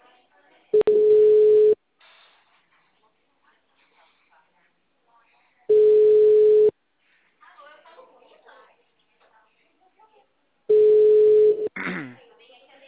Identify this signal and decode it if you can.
Ringback tone (Europe, 425 Hz), with speech.